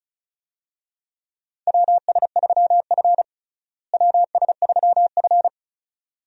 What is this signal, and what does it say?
Morse code, heard as a keyed tone, about 34 wpm, 700 Hz. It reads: WS3F WS3F
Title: USA Call Signs - 4 Characters - Repeated 2x - 35wpm